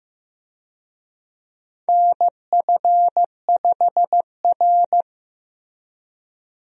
Morse code, heard as a keyed tone, about 15 wpm, 700 Hz. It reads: NF5R
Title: USA Call Signs - 4 Characters - Repeated 2x - 15wpm